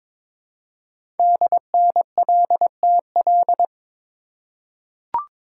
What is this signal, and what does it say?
Morse code, heard as a keyed tone, about 22 wpm, 700 Hz. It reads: DNLTL
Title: Code Groups - 5 Letters - 22wpm